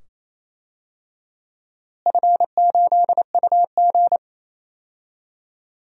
Morse code, heard as a keyed tone, about 28 wpm, 700 Hz. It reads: F8UG
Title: DXCC-100 Call Signs - 4 Characters - Repeated 2x - 28wpm